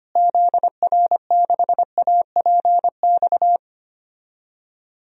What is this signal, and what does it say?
Morse code, heard as a keyed tone, about 25 wpm, 700 Hz. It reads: ZR6APX
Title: DXCC-100 Call Signs - 6 Characters - Repeated 2x - 25wpm